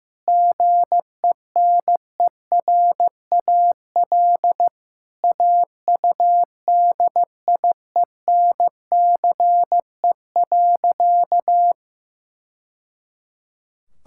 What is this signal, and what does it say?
Morse code, heard as a keyed tone, about 15 wpm, 700 Hz. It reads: GENERAL AUDIENCE.
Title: Sentences from Top 1000 Words - Review Part 5 - 15wpm